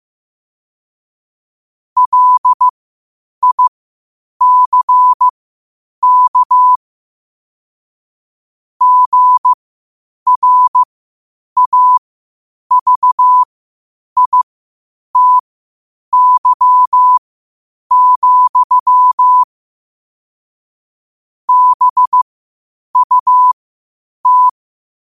Morse code, heard as a keyed tone, about 15 wpm, spaced as for 8 wpm, 1000 Hz. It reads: LICK GRAVITY, BUT
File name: qotdfiles / 8wpm